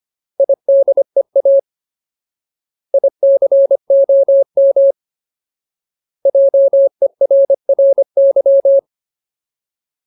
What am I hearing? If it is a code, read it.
Morse code, heard as a keyed tone, about 25 wpm, 550 Hz. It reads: IDEA ICOM JERRY
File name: Ultimate Word List / 25WPM Ultimate